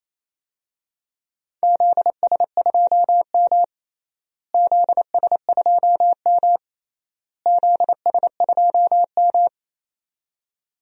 Morse code, heard as a keyed tone, about 27 wpm, 700 Hz. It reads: ZS2M ZS2M ZS2M
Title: DXCC-100 Call Signs - 4 Characters - Repeated 3x - 28wpm